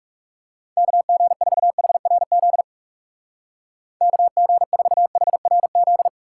Morse code, heard as a keyed tone, about 37 wpm, 700 Hz. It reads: KG4HRZ KG4HRZ
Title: Speed Racing - USA Call Signs 30wpm